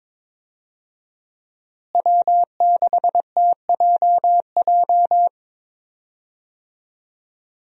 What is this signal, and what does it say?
Morse code, heard as a keyed tone, about 22 wpm, 690 Hz. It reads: W6TJJ
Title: USA Call Signs - 5 Characters - Repeated 3x - 22wpm